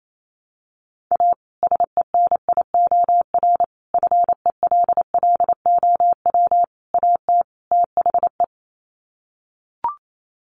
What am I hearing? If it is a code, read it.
Morse code, heard as a keyed tone, about 28 wpm, 700 Hz. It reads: A SENIOR FELLOW AT THE
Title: Sets of 5 Words Encore 28wpm